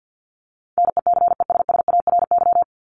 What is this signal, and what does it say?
Morse code, heard as a keyed tone, about 50 wpm, 700 Hz. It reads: NECESSARY
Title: Top 501-750 Words 50wpm